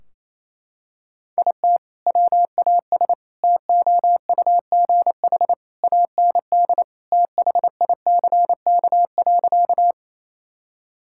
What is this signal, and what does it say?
Morse code, heard as a keyed tone, about 28 wpm, 700 Hz. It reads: IT WAS TOUGH AND THICK.
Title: Sentences from Top 1600 Words 28wpm